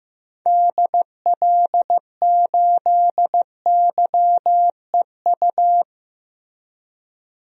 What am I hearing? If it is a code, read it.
Morse code, heard as a keyed tone, about 15 wpm, 700 Hz. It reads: DL8YEU
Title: DXCC-100 Call Signs - 6 Characters - Repeated 2x - 15wpm